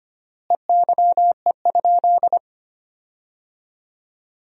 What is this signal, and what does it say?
Morse code, heard as a keyed tone, about 25 wpm, 700 Hz. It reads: EYE?
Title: 3x Word-Spacing - Sentences from Top 500 Words - 25wpm